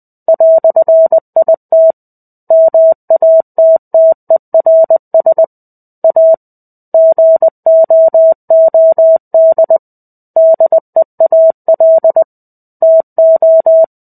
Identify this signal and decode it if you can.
Morse code, heard as a keyed tone, about 20 wpm, 650 Hz. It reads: "IT MATTERS A GOOD DEAL TO
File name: 05-Chapter-05 20 WPM